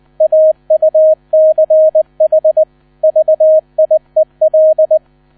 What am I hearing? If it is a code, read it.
Morse code, heard as a keyed tone, about 20 wpm, 630 Hz. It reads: AUCH VIEL
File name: 000 Vorwort 20wpm